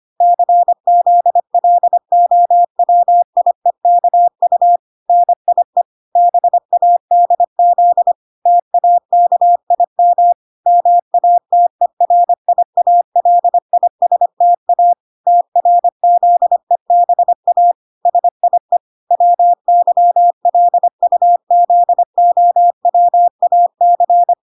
Morse code, heard as a keyed tone, about 25 wpm, 700 Hz. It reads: CZLOWIEKU NIE BADZ TAKIM MATERIALISTA TRZEBA SIE WYLUZOWAC